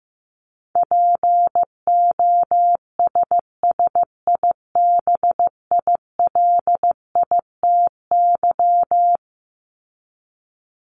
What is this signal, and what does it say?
Morse code, heard as a keyed tone, about 15 wpm, 700 Hz. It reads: POSSIBILITY
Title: Top 1751-2000 Words 15wpm